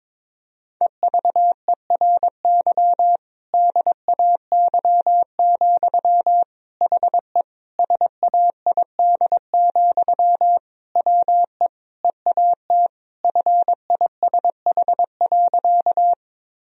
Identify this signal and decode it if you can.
Morse code, heard as a keyed tone, about 22 wpm, 700 Hz. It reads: EVERY DAY, HE SAID, WE EAT FISH.
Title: Sentences from Top 1000 Words - Review Part 1 - 22wpm